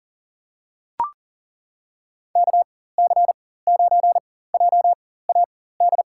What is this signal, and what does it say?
Morse code, heard as a keyed tone, about 39 wpm, spaced as for 18 wpm, 700 Hz. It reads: KC9JAD
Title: USA Call Signs - 6 Characters - 4x Character Spacing - 40wpm